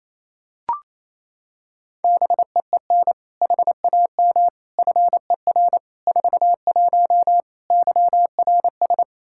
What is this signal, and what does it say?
Morse code, heard as a keyed tone, about 28 wpm, 700 Hz. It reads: BEEN HAM FER 41 YRS
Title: Practice QSO - Exchange 3 - More Info - 28wpm